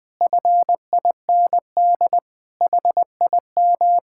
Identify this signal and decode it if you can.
Morse code, heard as a keyed tone, about 20 wpm, 700 Hz. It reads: FIND HIM
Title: Sentences from Top 1600 Words 20wpm